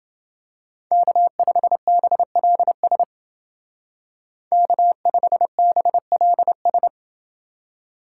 Morse code, heard as a keyed tone, about 29 wpm, 700 Hz. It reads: K5BLS K5BLS